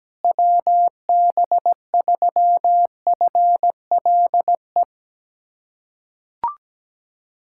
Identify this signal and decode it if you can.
Morse code, heard as a keyed tone, about 17 wpm, 700 Hz. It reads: WB3FLE
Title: USA Callsign DE Callsign 17wpm